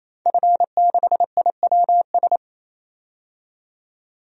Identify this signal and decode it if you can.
Morse code, heard as a keyed tone, about 28 wpm, 700 Hz. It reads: F6IWS
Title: DXCC-100 Call Signs - 5 Characters - 28wpm